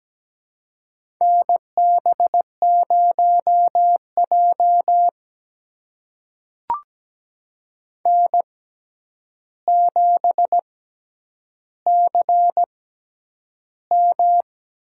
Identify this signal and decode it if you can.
Morse code, heard as a keyed tone, about 17 wpm, spaced as for 9 wpm, 700 Hz. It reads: NB0J N7CM
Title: USA Call Signs - 4 Characters - 6x Character Spacing - 17wpm